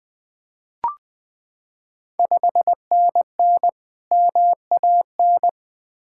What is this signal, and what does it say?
Morse code, heard as a keyed tone, about 20 wpm, 700 Hz. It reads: 5NN MAN
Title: Ontario QSO Party Exchange 20wpm